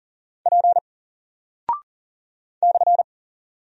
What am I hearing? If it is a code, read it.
Morse code, heard as a keyed tone, about 39 wpm, 700 Hz. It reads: P /
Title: Single Character (Letter:Number:Punctuation:Prosign) 40wpm